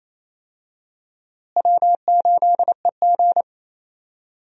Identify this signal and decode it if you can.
Morse code, heard as a keyed tone, about 28 wpm, 700 Hz. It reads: W8EG
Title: USA Call Signs - 4 Characters - Repeated 3x - 28wpm